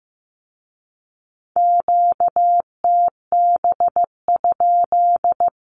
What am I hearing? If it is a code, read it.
Morse code, heard as a keyed tone, about 15 wpm, 700 Hz. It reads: QTB?
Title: Q-Codes with Spoken Meaning - 15wpm